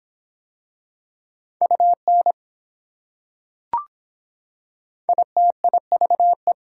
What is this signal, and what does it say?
Morse code, heard as a keyed tone, about 24 wpm, 700 Hz. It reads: UN ITIVE